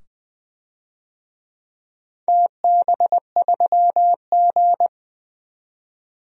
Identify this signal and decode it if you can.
Morse code, heard as a keyed tone, about 20 wpm, 700 Hz. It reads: TB3G
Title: DXCC-100 Call Signs - 4 Characters - 20wpm